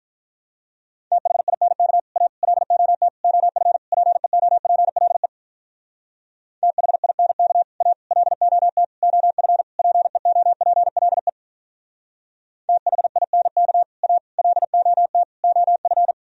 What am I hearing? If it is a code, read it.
Morse code, heard as a keyed tone, about 44 wpm, 700 Hz. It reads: THINK A LOT OF PEOPLE THINK A LOT OF PEOPLE THINK A LOT OF